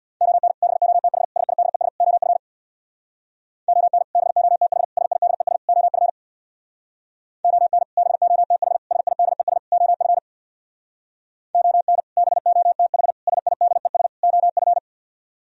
Morse code, heard as a keyed tone, about 64 wpm, 700 Hz. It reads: ON BOTH SIDES OF ON BOTH SIDES OF ON BOTH SIDES OF ON BOTH SIDES OF